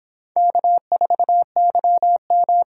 Morse code, heard as a keyed tone, about 24 wpm, 700 Hz. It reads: K4YM